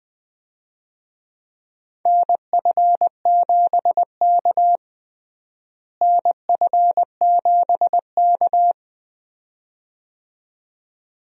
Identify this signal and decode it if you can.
Morse code, heard as a keyed tone, about 20 wpm, 700 Hz. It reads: NF7K NF7K
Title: USA Call Signs - 4 Characters - Repeated 2x - 20wpm